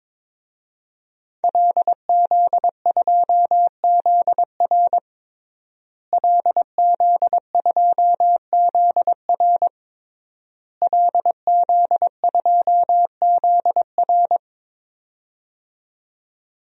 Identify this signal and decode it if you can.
Morse code, heard as a keyed tone, about 22 wpm, 700 Hz. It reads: LZ2ZR LZ2ZR LZ2ZR
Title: DXCC-100 Call Signs - 5 Characters - Repeated 3x - 22wpm